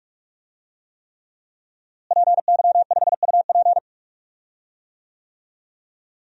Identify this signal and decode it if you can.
Morse code, heard as a keyed tone, about 46 wpm, 700 Hz. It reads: PY5UP
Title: DXCC-100 Call Signs - 5 Characters - Repeated 3x - 45wpm